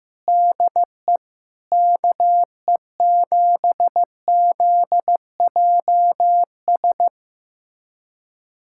Morse code, heard as a keyed tone, about 15 wpm, 700 Hz. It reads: DE KE7ZJS
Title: USA Callsign DE Callsign - 6 Characters - 15wpm